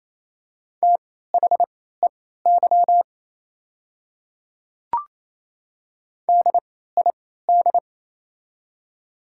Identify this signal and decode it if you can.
Morse code, heard as a keyed tone, about 28 wpm, spaced as for 16 wpm, 690 Hz. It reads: THEY DID
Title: 3x Character Spacing - Top 200 Words - 28wpm